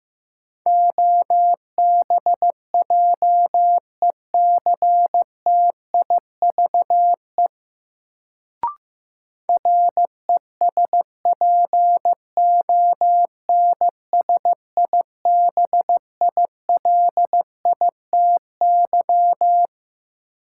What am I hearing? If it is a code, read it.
Morse code, heard as a keyed tone, about 15 wpm, 700 Hz. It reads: OBJECTIVE RESPONSIBILITY
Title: Sentences from Top 1200 Words - New Word Drill - 15wpm